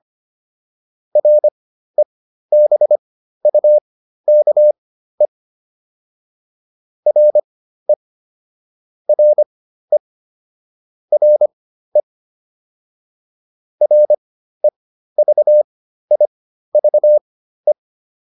Morse code, heard as a keyed tone, about 25 wpm, spaced as for 13 wpm, 600 Hz. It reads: REBUKE RE RE RE REVIVE